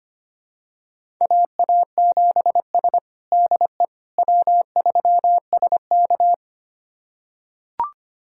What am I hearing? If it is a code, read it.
Morse code, heard as a keyed tone, about 25 wpm, 700 Hz. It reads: AA7S DE W3SK
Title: USA Callsign DE Callsign - 4 Characters - 25wpm